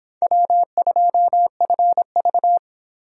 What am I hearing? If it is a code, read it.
Morse code, heard as a keyed tone, about 24 wpm, 700 Hz. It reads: W2FV